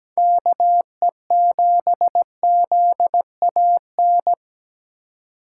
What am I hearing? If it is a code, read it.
Morse code, heard as a keyed tone, about 17 wpm, 700 Hz. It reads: KE7ZAN
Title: USA Call Signs - 2x Character Spacing - 17wpm